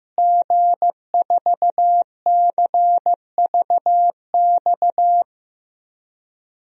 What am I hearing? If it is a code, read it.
Morse code, heard as a keyed tone, about 15 wpm, 700 Hz. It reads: G4CVX
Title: DXCC-100 Call Signs - 5 Characters - Repeated 2x - 15wpm